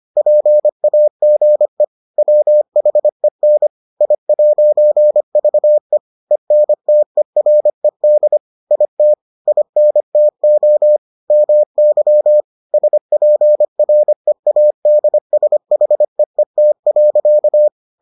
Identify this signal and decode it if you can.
Morse code, heard as a keyed tone, about 25 wpm, 600 Hz. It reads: PAGE WHEN I'VE ENTERED IT INTO MY SPREADSHEET.